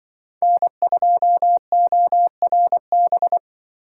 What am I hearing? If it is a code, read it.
Morse code, heard as a keyed tone, about 25 wpm, 700 Hz. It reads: N2ORB
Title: Speed Racing - USA 1x3 Call Signs 20wpm